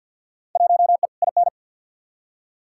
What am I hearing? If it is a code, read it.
Morse code, heard as a keyed tone, about 52 wpm, 700 Hz. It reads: 1E IN
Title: Field Day Exchanges - 50wpm